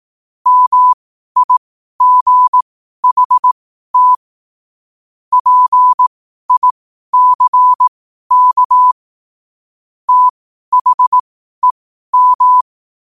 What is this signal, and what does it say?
Morse code, heard as a keyed tone, about 18 wpm, spaced as for 12 wpm, 1000 Hz. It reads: MIGHT PICK THEM
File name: 12wpm